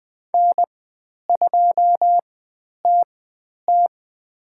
Morse code, heard as a keyed tone, about 20 wpm, spaced as for 10 wpm, 700 Hz. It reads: N2TT
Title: 095 Lesson 16g - Letter c - Call Signs 20wpm